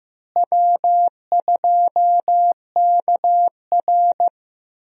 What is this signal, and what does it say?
Morse code, heard as a keyed tone, about 15 wpm, 700 Hz. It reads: W2KR
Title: USA Call Signs - 4 Characters - Repeated 2x - 15wpm